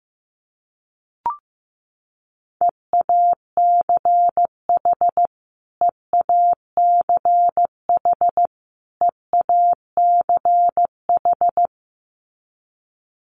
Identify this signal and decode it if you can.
Morse code, heard as a keyed tone, about 15 wpm, 700 Hz. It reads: EACH EACH EACH
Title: Top 500 Words Repeated 15wpm